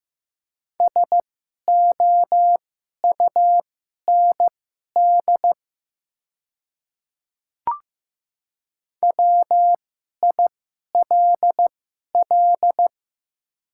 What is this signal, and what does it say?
Morse code, heard as a keyed tone, about 15 wpm, spaced as for 11 wpm, 700 Hz. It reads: SOUND WILL
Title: 2x Character Spacing - Top 100 Words - 15wpm